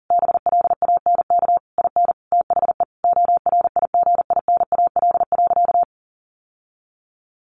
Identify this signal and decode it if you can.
Morse code, heard as a keyed tone, about 40 wpm, 700 Hz. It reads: BLANK IN THE ORIGINAL.
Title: Sentences from Top 1500 Words 40wpm